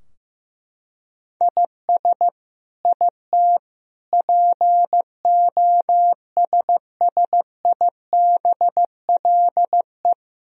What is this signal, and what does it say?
Morse code, heard as a keyed tone, about 15 wpm, 700 Hz. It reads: IS IT POSSIBLE
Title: Sentences from Top 1000 Words - Review Part 3 - 15wpm